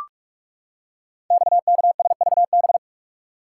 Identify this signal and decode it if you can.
Morse code, heard as a keyed tone, about 44 wpm, 700 Hz. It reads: XKSVB
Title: Code Groups - 5 Letters - 45wpm